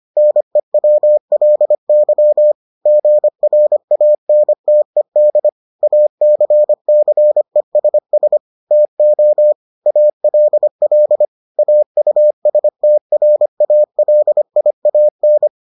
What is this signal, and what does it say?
Morse code, heard as a keyed tone, about 25 wpm, 600 Hz. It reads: NEWLY GRANTED ACCESS TO ALL AUSTRALIAN